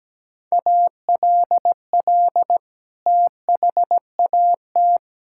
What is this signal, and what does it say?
Morse code, heard as a keyed tone, about 17 wpm, 700 Hz. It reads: ALL THAT
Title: Sentences from Top 1000 Words - Review Part 5 - 17wpm